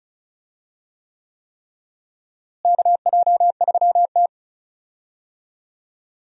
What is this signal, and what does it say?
Morse code, heard as a keyed tone, about 34 wpm, 700 Hz. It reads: KJ3T
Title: USA Call Signs - 4 Characters - Repeated 3x - 35wpm